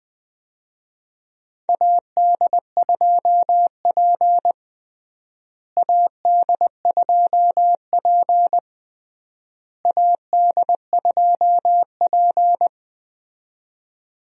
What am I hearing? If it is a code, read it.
Morse code, heard as a keyed tone, about 20 wpm, 700 Hz. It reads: AD2P AD2P AD2P
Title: USA Call Signs - 4 Characters - Repeated 3x - 20wpm